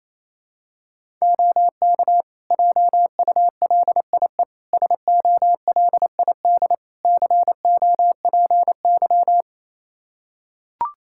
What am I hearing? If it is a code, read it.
Morse code, heard as a keyed tone, about 27 wpm, 700 Hz. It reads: OK JULIE SOLID COPY
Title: Practice QSO - Exchange 3 - More Info - 28wpm